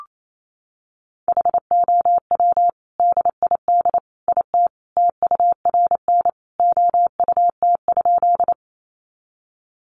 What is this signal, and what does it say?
Morse code, heard as a keyed tone, about 28 wpm, 700 Hz. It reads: HOW DID IT TURN OUT?
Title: A Sound Of Thunder 28wpm